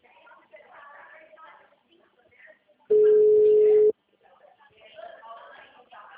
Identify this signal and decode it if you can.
Ringback tone (Europe, 425 Hz), with speech.